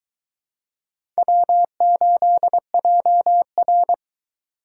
Morse code, heard as a keyed tone, about 21 wpm, 700 Hz. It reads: W8JR